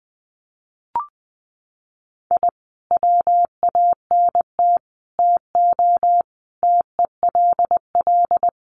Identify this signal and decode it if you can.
Morse code, heard as a keyed tone, about 20 wpm, 700 Hz. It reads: I WANT TO TELL
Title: Sets of 5 Words Encore 20wpm